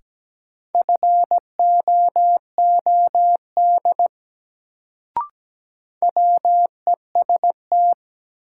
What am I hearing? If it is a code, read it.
Morse code, heard as a keyed tone, about 17 wpm, 700 Hz. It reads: FOOD WEST
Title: Sets of 5 Words - New Word Drill - 17wpm